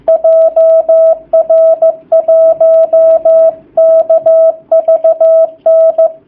Morse code, heard as a keyed tone, about 15 wpm, 660 Hz. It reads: JR1KVN
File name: jh8cbh / 2019cw / sp15- w